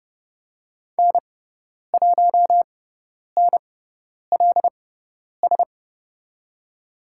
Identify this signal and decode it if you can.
Morse code, heard as a keyed tone, about 30 wpm, spaced as for 10 wpm, 700 Hz. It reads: N1NLS
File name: Lesson 16g - Letter c - Call Signs 30wpm